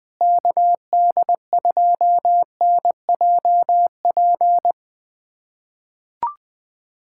Morse code, heard as a keyed tone, about 21 wpm, 700 Hz. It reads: KD2NJP
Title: USA Callsign DE Callsign - 6 Characters - 20wpm